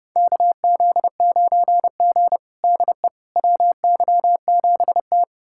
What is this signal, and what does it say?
Morse code, heard as a keyed tone, about 31 wpm, 700 Hz. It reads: KZ9G DE WY7T
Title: USA Callsign DE Callsign - 4 Characters - 30wpm